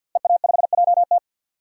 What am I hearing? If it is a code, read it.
Morse code, heard as a keyed tone, about 50 wpm, 700 Hz. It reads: EA5JT
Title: DXCC-100 Call Signs - 5 Characters - Repeated 3x - 50wpm